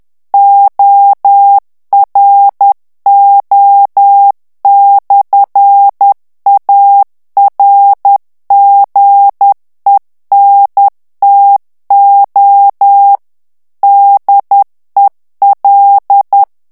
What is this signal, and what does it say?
Morse code, heard as a keyed tone, about 10 wpm, 800 Hz. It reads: ORO/ARGENTO DEL
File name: morse mp3 / ITALIEN texte clair 1 vitesse 10 mots minute